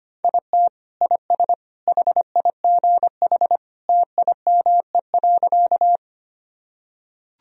Morse code, heard as a keyed tone, about 25 wpm, 700 Hz. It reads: IT IS HIGH TIME.
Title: Sentences from Top 1000 Words - Review Part 1 - 25wpm